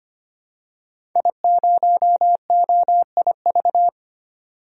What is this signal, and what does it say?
Morse code, heard as a keyed tone, about 25 wpm, 700 Hz. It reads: I0OIV
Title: DXCC-100 Call Signs - 5 Characters - 25wpm